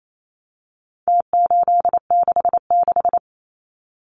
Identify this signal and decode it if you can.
Morse code, heard as a keyed tone, about 28 wpm, 700 Hz. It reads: T866
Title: Numbers - Contest Style - 4-Digit 28wpm